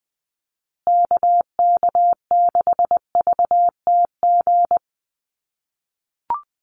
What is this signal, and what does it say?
Morse code, heard as a keyed tone, about 20 wpm, 700 Hz. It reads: KK6VTG
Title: USA Call Signs Repeated 2x 20wpm